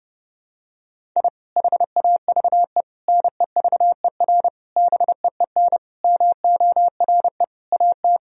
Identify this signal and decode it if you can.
Morse code, heard as a keyed tone, about 30 wpm, 700 Hz. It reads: I HAVE NEVER BEEN MORE AT
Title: Sentences from Top 1000 Words - Review Part 1 - 30wpm